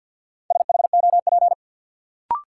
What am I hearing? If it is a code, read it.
Morse code, heard as a keyed tone, about 53 wpm, 700 Hz. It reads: SHOP
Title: Sentences from Top 1600 Words - New Word Drill Alt - 50wpm